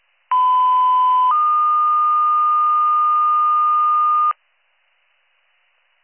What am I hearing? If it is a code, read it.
Two-tone page, heard as a one-second tone, then a three-second tone, about 1010 then 1220 Hz. It is heard from 0.3 to 4.3 s.